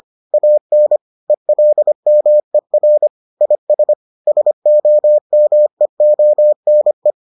Morse code, heard as a keyed tone, about 25 wpm, 600 Hz. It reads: AN ELMER IS SOMEONE